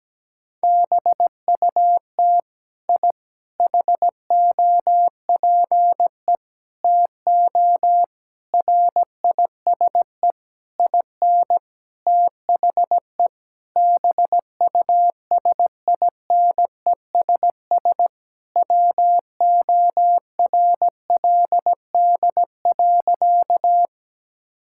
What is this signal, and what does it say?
Morse code, heard as a keyed tone, about 17 wpm, 700 Hz. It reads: BUT I HOPE TO RISE IN THE BUSINESS WORLD.
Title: Sentences from Top 1000 Words - Review Part 6 - 17wpm